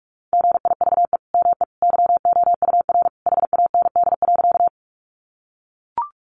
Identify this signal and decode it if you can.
Morse code, heard as a keyed tone, about 44 wpm, 700 Hz. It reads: GIVE ME YOUR HAND.
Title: Sentences from Top 200 Words 45wpm